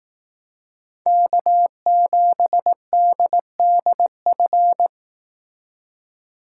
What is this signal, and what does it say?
Morse code, heard as a keyed tone, about 18 wpm, 700 Hz. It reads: K7DDF
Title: Speed Racing - USA 1x3 Call Signs 15wpm